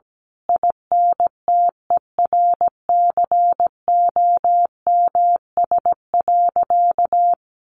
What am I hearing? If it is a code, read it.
Morse code, heard as a keyed tone, about 17 wpm, 700 Hz. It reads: INTERCOMS.
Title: A Sound Of Thunder 17wpm